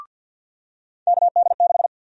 Morse code, heard as a keyed tone, about 50 wpm, 700 Hz. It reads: KD6
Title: USA Call Sign Prefixes - 50wpm